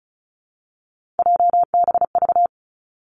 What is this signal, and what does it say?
Morse code, heard as a keyed tone, about 36 wpm, 700 Hz. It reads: JBV
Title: Three Challenging Letter-Number Combinations 35wpm